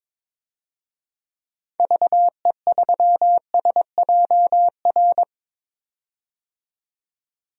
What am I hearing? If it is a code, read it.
Morse code, heard as a keyed tone, about 22 wpm, 700 Hz. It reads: VE3SJR
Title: DXCC-100 Call Signs - 6 Characters - 22wpm